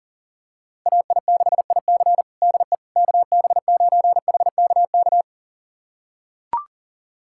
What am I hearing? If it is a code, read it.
Morse code, heard as a keyed tone, about 40 wpm, 700 Hz. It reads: AI6IC DE KB9HKK
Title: USA Callsign DE Callsign 40wpm